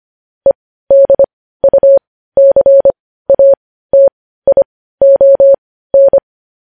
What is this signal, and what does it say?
Morse code, heard as a keyed tone, about 25 wpm, spaced as for 15 wpm, 550 Hz. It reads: EDUCATION
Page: news-headlines-in-morse-code-at-15-wpm